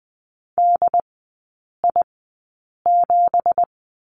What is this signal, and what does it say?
Morse code, heard as a keyed tone, about 20 wpm, spaced as for 8 wpm, 700 Hz. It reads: DI7
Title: 178 Lesson 29e - Slash - Three Characters 20wpm